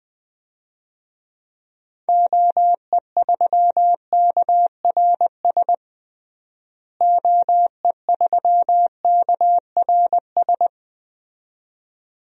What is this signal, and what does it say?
Morse code, heard as a keyed tone, about 20 wpm, 700 Hz. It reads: OE3KRS OE3KRS
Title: DXCC-100 Call Signs - 6 Characters - Repeated 2x - 20wpm